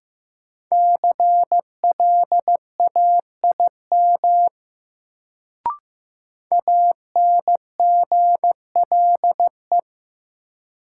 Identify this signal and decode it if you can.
Morse code, heard as a keyed tone, about 15 wpm, 700 Hz. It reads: CLAIM ANGLE
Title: Sentences from Top 1500 Words - New Word Drill Alt - 15wpm